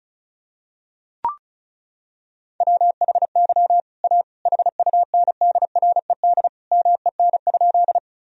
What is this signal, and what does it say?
Morse code, heard as a keyed tone, about 36 wpm, 700 Hz. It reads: WHY A HUNDRED MEN?
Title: Sentences from Top 1000 Words - Review Part 2 - 35wpm